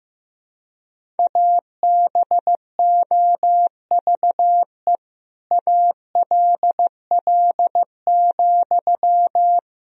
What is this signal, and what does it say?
Morse code, heard as a keyed tone, about 16 wpm, 700 Hz. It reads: ABOVE ALL,
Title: Sentences from Top 1000 Words - Review Part 4 - 15wpm